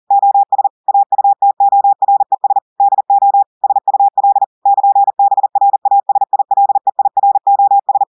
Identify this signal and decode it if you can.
Morse code, heard as a keyed tone, about 40 wpm, 800 Hz. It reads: OS AUTORES DO SUL <KN>BRASILEIROS
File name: audios / portugais / portugais texte en clair 1 vitesse 40 mots minutes